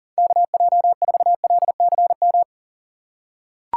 Morse code, heard as a keyed tone, about 40 wpm, 700 Hz. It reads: KJ4LCM
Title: USA Call Signs - 6 Characters - 4x Character Spacing - 40wpm